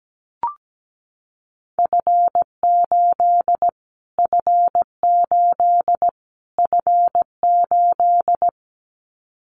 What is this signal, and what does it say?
Morse code, heard as a keyed tone, about 17 wpm, 700 Hz. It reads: F8 F8 F8
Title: Callsign Prefixes for Top 25 DXCC - Repeated 3x - 17wpm